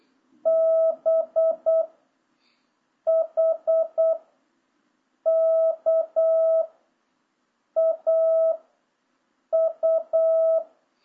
Morse code, heard as a keyed tone, about 8 wpm, 650 Hz. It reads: B H K A U